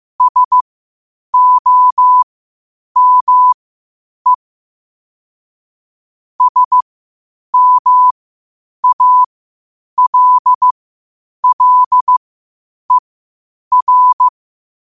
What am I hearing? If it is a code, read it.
Morse code, heard as a keyed tone, about 15 wpm, spaced as for 8 wpm, 1000 Hz. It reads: SOME SMALLER
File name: qotdfiles / 8wpm